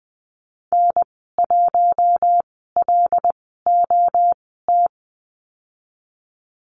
Morse code, heard as a keyed tone, about 20 wpm, 700 Hz. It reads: N 1 L O T
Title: 207 Lesson 33g - Letter j - Call Signs 20wpm